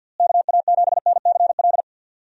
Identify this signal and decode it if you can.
Morse code, heard as a keyed tone, about 49 wpm, 700 Hz. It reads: KA7NCL
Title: USA Call Signs - 6 Characters - Repeated 3x - 50wpm